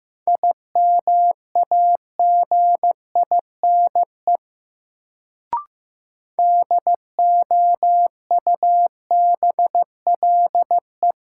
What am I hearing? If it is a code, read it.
Morse code, heard as a keyed tone, about 15 wpm, 700 Hz. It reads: IMAGINE DOUBLE
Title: Sentences from Top 1400 Words - New Word Drill Alt - 15wpm